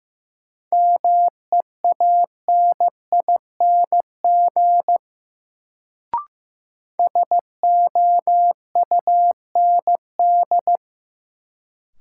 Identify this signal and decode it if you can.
Morse code, heard as a keyed tone, about 15 wpm, 700 Hz. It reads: MEANING SOUND
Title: Sentences from Top 900 Words - New Word Drill - 15wpm